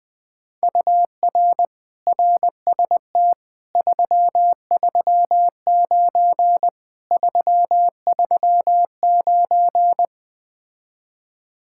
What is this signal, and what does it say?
Morse code, heard as a keyed tone, about 20 wpm, 700 Hz. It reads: UR RST 339 339
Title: Speed-Racing Common QSO Phrases - 17wpm